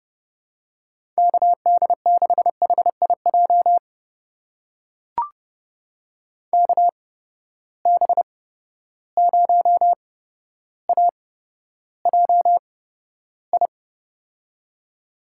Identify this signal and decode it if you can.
Morse code, heard as a keyed tone, about 31 wpm, spaced as for 13 wpm, 700 Hz. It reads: KD6HIJ KB0AJI
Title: USA Call Signs - 6 Characters - 8x Character Spacing - 30wpm